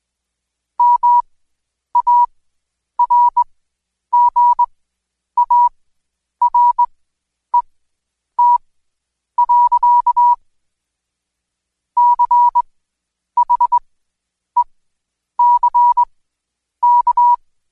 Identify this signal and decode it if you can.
Morse code, heard as a keyed tone, about 21 wpm, spaced as for 9 wpm, 970 Hz. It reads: MARGARET. CHECK
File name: Code Quick / AudioDownld2 / Seg116-Day21-DayDrill 21